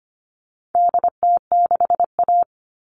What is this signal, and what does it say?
Morse code, heard as a keyed tone, about 25 wpm, 700 Hz. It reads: DT6A
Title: All Cut Numbers - 4-Digit 25wpm